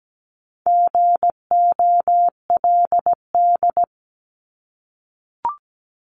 Morse code, heard as a keyed tone, about 17 wpm, 700 Hz. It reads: GOLD
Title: Top 500 Words 17wpm